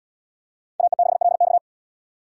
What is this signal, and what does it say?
Morse code, heard as a keyed tone, about 72 wpm, 700 Hz. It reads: AE6LX